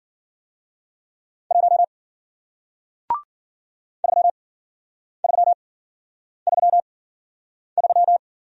Morse code, heard as a keyed tone, about 54 wpm, spaced as for 9 wpm, 700 Hz. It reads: 1 3333